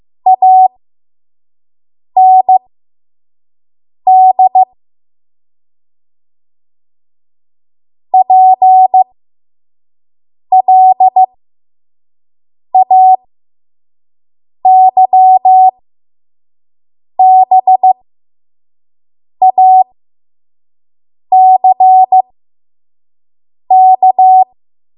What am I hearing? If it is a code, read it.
Morse code, heard as a keyed tone, about 15 wpm, spaced as for 5 wpm, 750 Hz. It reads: AND PLAYBACK
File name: Morse / 05WPM